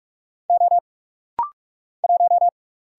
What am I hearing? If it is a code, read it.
Morse code, heard as a keyed tone, about 46 wpm, 700 Hz. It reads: O 1
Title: Single Letter-Number - Rapid-Fire 45wpm